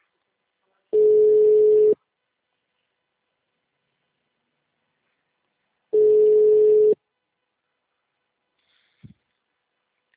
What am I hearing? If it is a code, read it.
Ringback tone (Europe, 425 Hz).
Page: Campo Limpo - SP